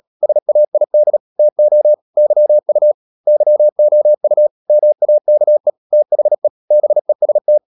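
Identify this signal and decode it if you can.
Morse code, heard as a keyed tone, about 37 wpm, 600 Hz. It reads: SAID TO YU YOU MAKE THE BEST
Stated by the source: Morse code practice at 35–39 wpm sending English text of Ken Follett's novel Never, from CW med Gnister.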